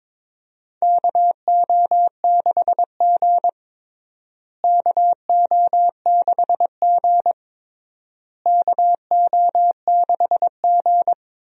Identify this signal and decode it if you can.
Morse code, heard as a keyed tone, about 23 wpm, 700 Hz. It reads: KO6G KO6G KO6G
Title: USA Call Signs - 4 Characters - Repeated 3x - 22wpm